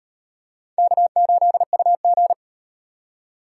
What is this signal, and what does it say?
Morse code, heard as a keyed tone, about 36 wpm, 700 Hz. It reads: K8UG